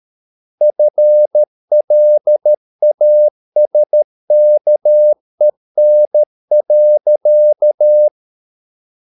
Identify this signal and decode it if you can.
Morse code, heard as a keyed tone, about 13 wpm, 600 Hz. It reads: FLASKEN.
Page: Ferie 13 wpm